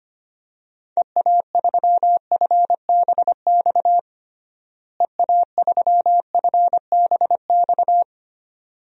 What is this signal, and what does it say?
Morse code, heard as a keyed tone, about 25 wpm, 700 Hz. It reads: EA3FBX EA3FBX
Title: DXCC-100 Call Signs - 6 Characters - Repeated 2x - 25wpm